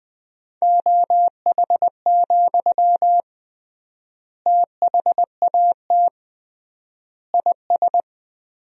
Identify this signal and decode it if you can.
Morse code, heard as a keyed tone, about 21 wpm, 700 Hz. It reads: OH, THAT IS
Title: 3x Word-Spacing - Sentences from Top 500 Words - 20wpm